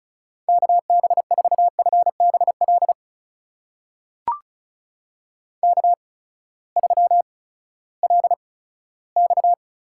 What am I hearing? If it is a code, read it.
Morse code, heard as a keyed tone, about 34 wpm, spaced as for 17 wpm, 700 Hz. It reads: KB4FBL K3LX
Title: USA Call Signs - 8x Character Spacing - 35wpm